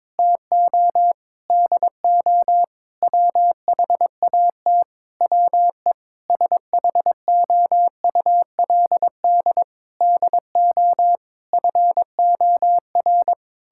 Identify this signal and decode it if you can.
Morse code, heard as a keyed tone, about 22 wpm, 700 Hz. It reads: TO DO WHAT WE SHOULD DO FOR
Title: Sentences from Top 1600 Words 22wpm